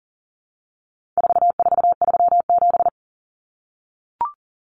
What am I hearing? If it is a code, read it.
Morse code, heard as a keyed tone, about 40 wpm, 700 Hz. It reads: 4437
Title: Numbers - 4-Digit 40wpm